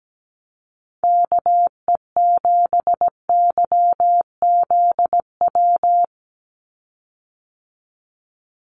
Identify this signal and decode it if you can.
Morse code, heard as a keyed tone, about 17 wpm, 700 Hz. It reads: KE7YZW
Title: USA Call Signs 17wpm